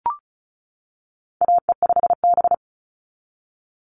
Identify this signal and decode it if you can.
Morse code, heard as a keyed tone, about 34 wpm, 700 Hz. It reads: AE5B
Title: NA SOTA Chaser Callsigns 35wpm